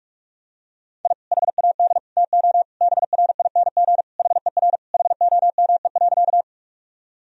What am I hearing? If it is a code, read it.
Morse code, heard as a keyed tone, about 46 wpm, 700 Hz. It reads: I HAD TO BRING HER HOME.
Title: Sentences from Top 1000 Words - Review Part 3 - 45wpm